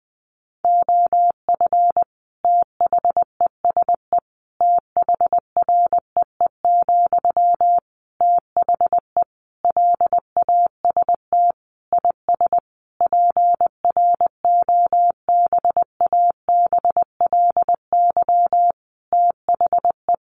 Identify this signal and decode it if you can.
Morse code, heard as a keyed tone, about 20 wpm, 700 Hz. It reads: OF THESE THREE, THE LAST IS PROBABLY THE
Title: Sentences from Top 700 Words 20wpm